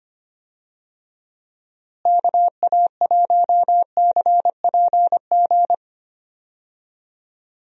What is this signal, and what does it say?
Morse code, heard as a keyed tone, about 25 wpm, 700 Hz. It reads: KA1CPG
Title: USA Call Signs - 6 Characters - Repeated 2x - 25wpm